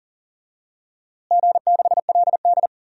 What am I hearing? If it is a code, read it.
Morse code, heard as a keyed tone, about 39 wpm, 700 Hz. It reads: G6LD
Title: DXCC-100 Call Signs - 4 Characters - Repeated 3x - 40wpm